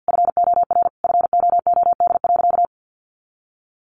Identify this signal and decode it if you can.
Morse code, heard as a keyed tone, about 50 wpm, 700 Hz. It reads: FOR FOOD.
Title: A Sound Of Thunder 50wpm